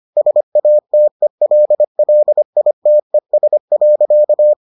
Morse code, heard as a keyed tone, about 25 wpm, 600 Hz. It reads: SATELLITES.